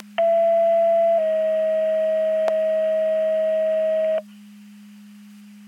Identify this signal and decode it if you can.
Two-tone page, heard as a one-second tone, then a three-second tone, about 670 then 640 Hz. It is heard from 0.2 to 4.2 s.
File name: Portage 154.385 Paging